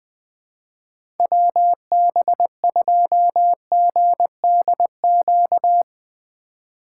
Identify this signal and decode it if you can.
Morse code, heard as a keyed tone, about 20 wpm, 700 Hz. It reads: WB2GDQ
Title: USA Call Signs - 6 Characters - 3x Character Spacing - 20wpm